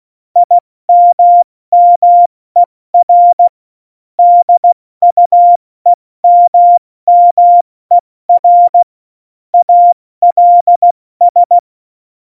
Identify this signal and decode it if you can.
Morse code, heard as a keyed tone, about 16 wpm, 700 Hz. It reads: IMMER DUEMMER ALS
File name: Morsekurs / Witze / Witz 04 Juli 17, Tempo70 F80